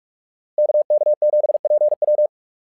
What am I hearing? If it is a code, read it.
Morse code, heard as a keyed tone, about 44 wpm, 600 Hz. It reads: KK7PW
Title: SOTA QSO Practice 45wpm